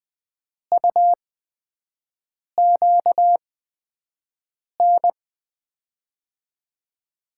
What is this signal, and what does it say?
Morse code, heard as a keyed tone, about 20 wpm, spaced as for 5 wpm, 700 Hz. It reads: UQN
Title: USA Call Signs - 6 Characters - 8x Character Spacing - 20wpm